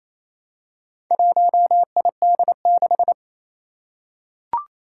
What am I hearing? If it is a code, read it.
Morse code, heard as a keyed tone, about 28 wpm, 700 Hz. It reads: 1ID6
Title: Four Challenging Letter-Number Combinations 28wpm